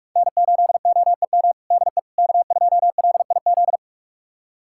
Morse code, heard as a keyed tone, about 46 wpm, 700 Hz. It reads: N8OEM DE K2LIZ
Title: USA Callsign DE Callsign 45wpm